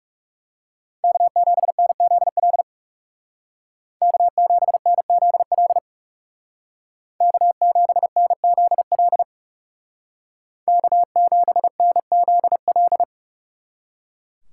Morse code, heard as a keyed tone, about 37 wpm, 700 Hz. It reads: K7NZL K7NZL K7NZL K7NZL
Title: Speed Racing - USA Call Signs 30wpm